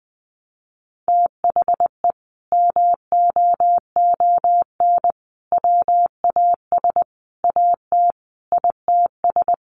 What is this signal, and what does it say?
Morse code, heard as a keyed tone, about 20 wpm, 700 Hz. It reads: THE MOON WAS AT ITS
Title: Sentences from Top 1300 Words 20wpm